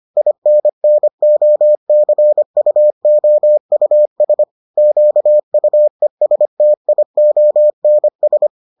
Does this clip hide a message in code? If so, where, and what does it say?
Yes: Morse code from 0.2 to 8.5 s, heard as a keyed tone, about 25 wpm, 600 Hz. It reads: INNOCUOUS QUESTIONS